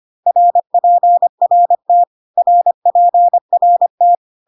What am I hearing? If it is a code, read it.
Morse code, heard as a keyed tone, about 25 wpm, 700 Hz. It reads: RPRT RPRT
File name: CW abbreviations alphabetical repeated 3 times 25 25wpm